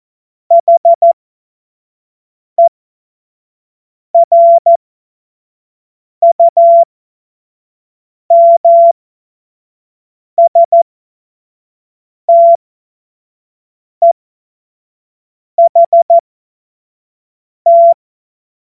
Morse code, heard as a keyed tone, about 14 wpm, spaced as for 5 wpm, 680 Hz. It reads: HERUMSTEHT
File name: Morsekurs / Witze / Witz 15. November 2022 25 F70